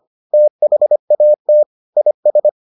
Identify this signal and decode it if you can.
Morse code, heard as a keyed tone, about 25 wpm, 600 Hz. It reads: THAT IS